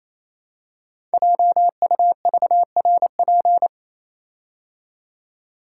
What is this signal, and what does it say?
Morse code, heard as a keyed tone, about 28 wpm, 700 Hz. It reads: JUVRP
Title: Code Groups - 5 Letters - 28wpm